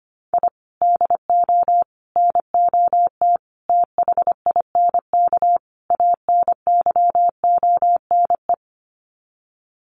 Morse code, heard as a keyed tone, about 25 wpm, 700 Hz. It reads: I DO NOT THINK ANYONE
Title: Sets of 5 Words Encore 25wpm